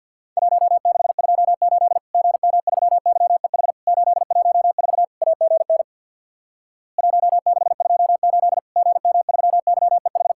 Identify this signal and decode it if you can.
Morse code, heard as a keyed tone, about 50 wpm, 700 Hz. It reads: 1628 GM3YEH 814 AGN 1628 GM3YEH
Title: Worked All Europe DX Contest - QTC Exchange with Repeats - 50wpm